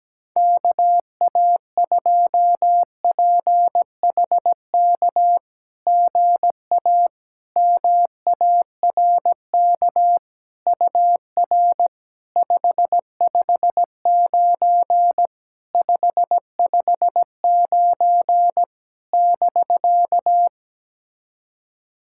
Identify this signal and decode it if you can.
Morse code, heard as a keyed tone, about 17 wpm, 690 Hz. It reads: KA2PHK GA MARK UR 559 559 <BK>
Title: SOTA QSO Practice 17wpm